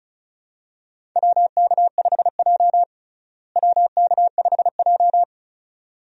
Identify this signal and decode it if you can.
Morse code, heard as a keyed tone, about 36 wpm, 700 Hz. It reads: WK5J WK5J
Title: USA Call Signs - 4 Characters - Repeated 2x - 35wpm